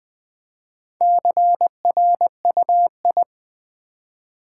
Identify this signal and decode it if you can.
Morse code, heard as a keyed tone, about 20 wpm, 700 Hz. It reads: CRUI
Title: Build A Word - 2 Letters At A Time - 6-Letter Words - 20wpm